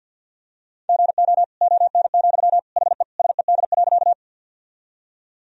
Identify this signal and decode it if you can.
Morse code, heard as a keyed tone, about 49 wpm, 700 Hz. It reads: GO ON, HE SED.
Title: Sentences from Top 100 Words and Abbreviations 50wpm